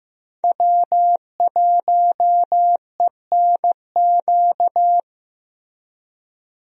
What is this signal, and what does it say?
Morse code, heard as a keyed tone, about 15 wpm, 700 Hz. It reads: W1ENQ
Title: USA Call Signs - 5 Characters - Repeated 3x - 15wpm